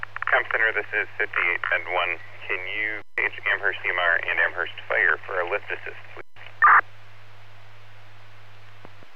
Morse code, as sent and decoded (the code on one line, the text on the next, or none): none